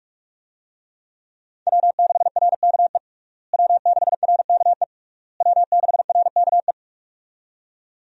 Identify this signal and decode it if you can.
Morse code, heard as a keyed tone, about 42 wpm, 700 Hz. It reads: W6RKE W6RKE W6RKE
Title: USA Call Signs - 5 Characters - Repeated 3x - 45wpm